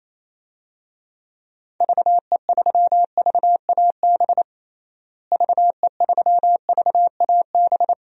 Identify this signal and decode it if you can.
Morse code, heard as a keyed tone, about 28 wpm, 700 Hz. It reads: VE3VAB VE3VAB
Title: DXCC-100 Call Signs - 6 Characters - Repeated 3x - 28wpm